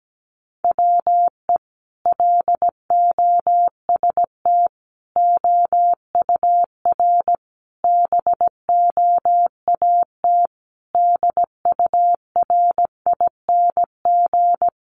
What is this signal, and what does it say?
Morse code, heard as a keyed tone, about 17 wpm, 700 Hz. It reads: WE LOST OUR BOAT DURING
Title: Sentences from Top 1500 Words 17wpm